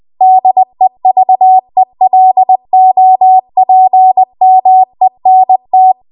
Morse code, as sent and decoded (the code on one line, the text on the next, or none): -.. ...- . .-.. --- .--. -- . -. -
DEVELOPMENT